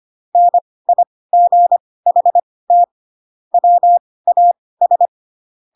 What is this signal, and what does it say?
Morse code, heard as a keyed tone, about 25 wpm, spaced as for 18 wpm, 700 Hz. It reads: NIGHT WAS
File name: Pinocchio / Pinocchio ch6 12 25 ik1hge